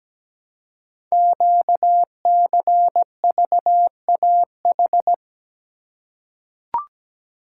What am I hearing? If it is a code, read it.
Morse code, heard as a keyed tone, about 17 wpm, 700 Hz. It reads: QCVAH
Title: Code Groups - 5 Letters - 17wpm